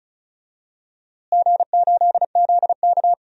Morse code, heard as a keyed tone, about 34 wpm, 700 Hz. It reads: G8ZK
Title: DXCC-100 Call Signs - 4 Characters - Repeated 2x - 35wpm